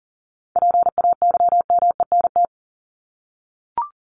Morse code, heard as a keyed tone, about 40 wpm, 700 Hz. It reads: PAYMENT
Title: Top 1001-1250 Words 40wpm